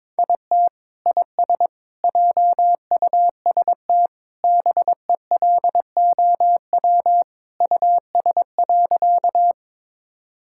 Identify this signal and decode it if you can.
Morse code, heard as a keyed tone, about 21 wpm, 700 Hz. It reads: IT IS JUST BELOW US.
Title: Sentences from Top 1000 Words - Review Part 5 - 22wpm